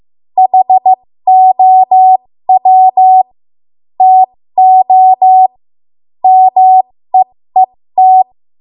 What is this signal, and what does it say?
Morse code, heard as a keyed tone, about 15 wpm, spaced as for 13 wpm, 750 Hz. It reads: HOW TO MEET
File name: Morse / 13WPM